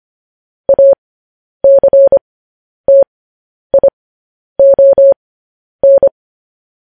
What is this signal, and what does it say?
Morse code, heard as a keyed tone, about 25 wpm, spaced as for 10 wpm, 550 Hz. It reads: ACTION
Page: news-headlines-in-morse-code-at-10-wpm